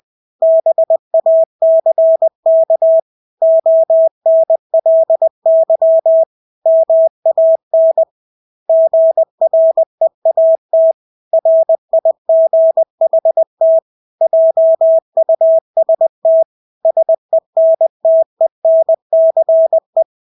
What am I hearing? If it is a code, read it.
Morse code, heard as a keyed tone, about 20 wpm, 650 Hz. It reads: BACK ONLY MAN GREAT RIGHT JUST SENTENCE
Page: Engelske ord 20 wpm